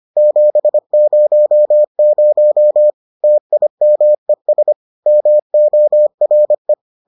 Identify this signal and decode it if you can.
Morse code, heard as a keyed tone, about 25 wpm, 600 Hz. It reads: 700 TIMES MORE